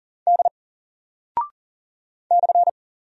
Morse code, heard as a keyed tone, about 40 wpm, 700 Hz. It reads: D /
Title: Single Character (Letter:Number:Punctuation:Prosign) 40wpm